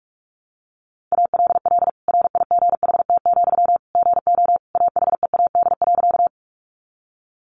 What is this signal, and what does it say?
Morse code, heard as a keyed tone, about 46 wpm, 700 Hz. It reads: ALL RIGHT, GO AHEAD.
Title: Sentences from Top 1100 Words 45wpm